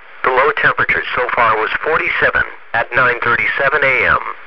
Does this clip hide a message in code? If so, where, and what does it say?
Yes: Emergency Alert System (SAME) tones from 0.5 to 4.2 s, heard as harsh buzzing bursts, with speech.